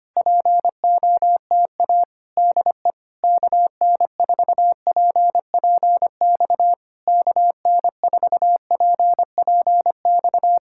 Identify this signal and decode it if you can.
Morse code, heard as a keyed tone, about 24 wpm, 700 Hz. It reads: POTA DE KN4PPX KN4PPX
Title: Speed-Racing Common QSO Phrases - 25wpm